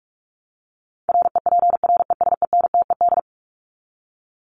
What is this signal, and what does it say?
Morse code, heard as a keyed tone, about 44 wpm, 700 Hz. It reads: REPRESENTED
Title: ED Suffix - Top 100 Words with Base Words 45wpm